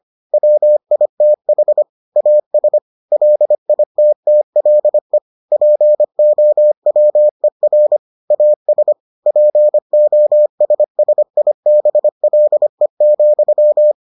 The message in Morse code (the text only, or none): WITH AS LITTLE POWER AS POSSIBLE,